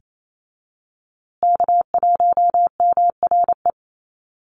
Morse code, heard as a keyed tone, about 28 wpm, 700 Hz. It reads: K1MRE
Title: USA Call Signs Repeated 2x 28wpm